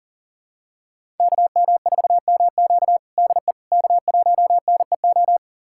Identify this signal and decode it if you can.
Morse code, heard as a keyed tone, about 40 wpm, 700 Hz. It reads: KM4MQ DE K1NEO
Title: USA Callsign DE Callsign - 5 Characters - 40wpm